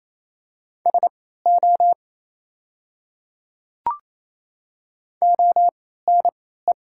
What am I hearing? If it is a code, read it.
Morse code, heard as a keyed tone, about 28 wpm, spaced as for 16 wpm, 700 Hz. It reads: SO ONE
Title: 3x Character Spacing - Top 100 Words - 28wpm